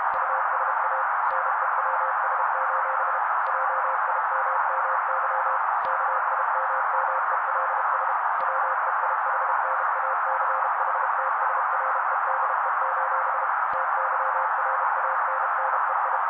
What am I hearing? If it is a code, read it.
Morse code, heard as a keyed tone, about 32 wpm, 530 Hz. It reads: USA NEWS: JIMMY KIMMELS WEIHNACHTSREDE: TYRANNEI
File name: daily / 31wpm noise